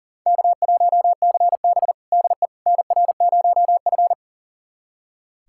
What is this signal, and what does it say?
Morse code, heard as a keyed tone, about 40 wpm, 700 Hz. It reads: K1CB DE NR0F
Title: USA Callsign DE Callsign - 4 Characters - 40wpm